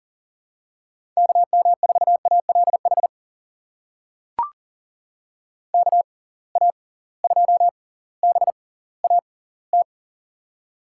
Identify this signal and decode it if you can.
Morse code, heard as a keyed tone, about 40 wpm, spaced as for 20 wpm, 700 Hz. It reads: KM4ALH KA2BAT
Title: USA Call Signs - 6 Characters - 6x Character Spacing - 40wpm